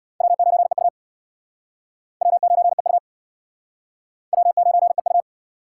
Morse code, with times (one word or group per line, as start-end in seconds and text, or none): 0.2-0.9 W9EU
2.2-3.0 W9EU
4.3-5.2 W9EU